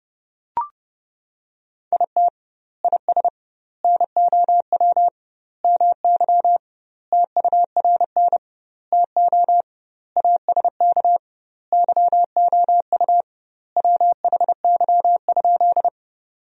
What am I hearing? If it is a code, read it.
Morse code, heard as a keyed tone, about 30 wpm, 700 Hz. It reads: IT IS NOW MY TURN TO ASK YOU WHY?
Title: 2x Word-Spacing - Sentences from Top 200 Words - 30wpm